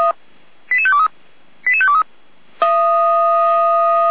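Five-tone selective call (ZVEI), heard as quick runs of five tones, tones of about 80 ms. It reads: 88523 88523
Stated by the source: ZVEI five-tone selective call received by SDR, served by Feuerwehr Orlamünde.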